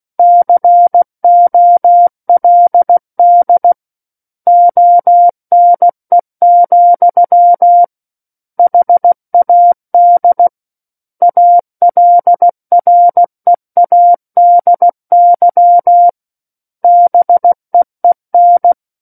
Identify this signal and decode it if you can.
Morse code, heard as a keyed tone, about 16 wpm, 700 Hz. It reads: COLD ONE, HAD ALREADY BEEN
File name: Book One / 07-Chapter 7 - 16 WPM